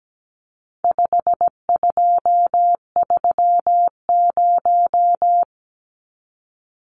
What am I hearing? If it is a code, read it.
Morse code, heard as a keyed tone, about 17 wpm, 700 Hz. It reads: 5230
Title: Numbers - 4-Digit 17wpm